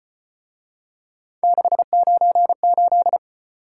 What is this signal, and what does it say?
Morse code, heard as a keyed tone, about 33 wpm, 700 Hz. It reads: 698